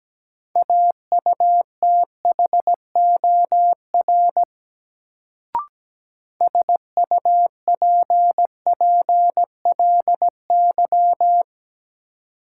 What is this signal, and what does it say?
Morse code, heard as a keyed tone, about 17 wpm, 700 Hz. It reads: AUTHOR SUPPLY
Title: Sentences from Top 1200 Words - New Word Drill Alt - 17wpm